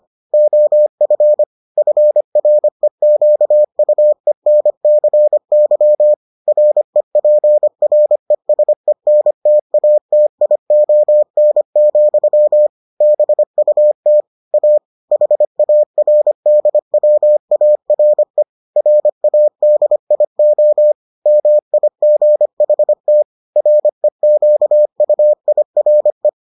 Morse code, heard as a keyed tone, about 25 wpm, 600 Hz. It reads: OF FREQUENCY REPRESENTATION, BUT A HARDWARE RADIO MIGHT REQUIRE